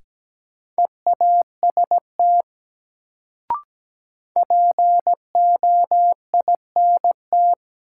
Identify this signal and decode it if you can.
Morse code, heard as a keyed tone, about 17 wpm, 700 Hz. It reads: EAST POINT
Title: Sets of 3 Words - New Word Drill - 17wpm